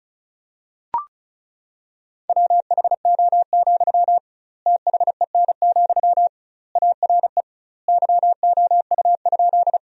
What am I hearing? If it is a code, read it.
Morse code, heard as a keyed tone, about 35 wpm, 700 Hz. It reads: WHO, THEN, ARE YOU?
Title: 2x Word-Spacing - Sentences from Top 100 Words - 35wpm